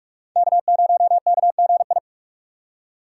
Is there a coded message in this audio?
Yes: Morse code from 0.4 to 2.0 s, heard as a keyed tone, about 44 wpm, 700 Hz. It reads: K0KGI